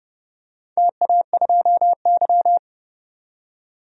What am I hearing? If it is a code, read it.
Morse code, heard as a keyed tone, about 29 wpm, 700 Hz. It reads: TA2Y
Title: DXCC-100 Call Signs - 4 Characters - 30wpm